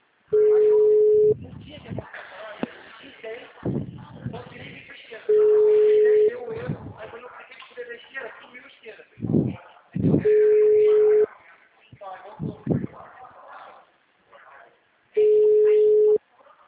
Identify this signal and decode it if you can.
Ringback tone (Europe, 425 Hz), with speech.
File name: out- -2011- - - .27